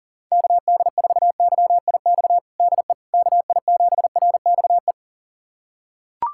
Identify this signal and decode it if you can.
Morse code, heard as a keyed tone, about 40 wpm, 700 Hz. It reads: KD4YIX DE KI7RXE
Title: USA Callsign DE Callsign - 6 Characters - 40wpm